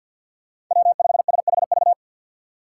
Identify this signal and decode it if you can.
Morse code, heard as a keyed tone, about 48 wpm, 700 Hz. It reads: W5SHV
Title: USA Call Signs - 6x Character Spacing - 50wpm